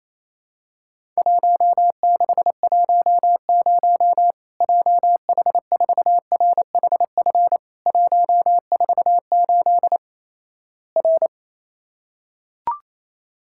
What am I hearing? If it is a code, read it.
Morse code, heard as a keyed tone, about 27 wpm, 700 Hz. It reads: 1610 JH4RHF 148 R
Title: Worked All Europe DX Contest - QTC Exchange - 28wpm